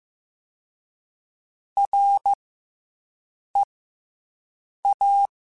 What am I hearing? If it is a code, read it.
Morse code, heard as a keyed tone, about 15 wpm, spaced as for 6 wpm, 800 Hz. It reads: REA